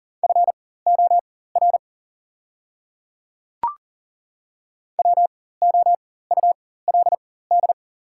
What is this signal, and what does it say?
Morse code, heard as a keyed tone, about 40 wpm, spaced as for 19 wpm, 690 Hz. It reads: FOR WOULD
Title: 4x Character Spacing - Top 100 Words - 40wpm